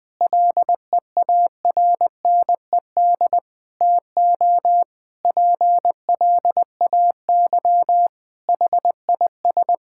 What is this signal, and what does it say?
Morse code, heard as a keyed tone, about 20 wpm, 700 Hz. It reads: LEARNED TO PLAY HIS
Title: Difficult Sentences with Hints - 20wpm